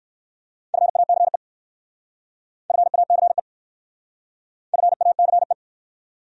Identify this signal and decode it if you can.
Morse code, heard as a keyed tone, about 53 wpm, 700 Hz. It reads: FACE FACE FACE
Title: Speed-Racing Top 300 Words 45wpm